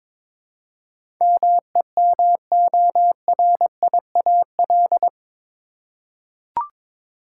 Morse code, heard as a keyed tone, about 21 wpm, 700 Hz. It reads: MEMORIAL
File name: practice / 22wpm / Eight Letter Words - 22wpm